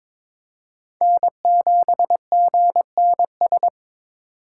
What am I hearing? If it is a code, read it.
Morse code, heard as a keyed tone, about 21 wpm, 700 Hz. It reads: N7GNS
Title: USA Call Signs - 4x Character Spacing - 22wpm